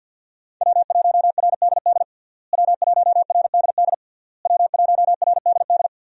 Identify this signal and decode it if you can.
Morse code, heard as a keyed tone, about 51 wpm, 700 Hz. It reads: W1RDD W1RDD W1RDD
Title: USA Call Signs - 5 Characters - Repeated 3x - 50wpm